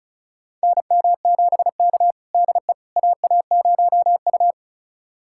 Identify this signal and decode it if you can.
Morse code, heard as a keyed tone, about 34 wpm, 700 Hz. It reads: NM7K DE AA0U
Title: USA Callsign DE Callsign - 4 Characters - 35wpm